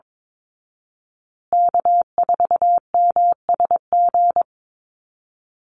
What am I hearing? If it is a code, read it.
Morse code, heard as a keyed tone, about 22 wpm, 700 Hz. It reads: K4MSG
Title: Call Signs Repeated 2x - 22wpm